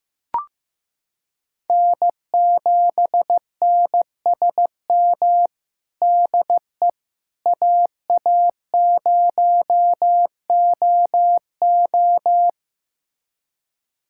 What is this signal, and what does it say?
Morse code, heard as a keyed tone, about 16 wpm, 700 Hz. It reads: N7NSM DE AA0OO
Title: USA Callsign DE Callsign - 5 Characters - 15wpm